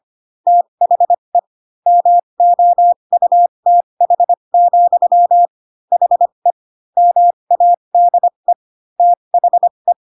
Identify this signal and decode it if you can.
Morse code, heard as a keyed tone, about 25 wpm, spaced as for 22 wpm, 700 Hz. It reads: THE MOUTH, HE MADE THE
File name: Pinocchio / Pinocchio ch3 18 25 ik1hge